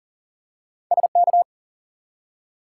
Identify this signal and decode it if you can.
Morse code, heard as a keyed tone, about 40 wpm, 700 Hz. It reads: SK
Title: US States and Canadian Provinces 40wpm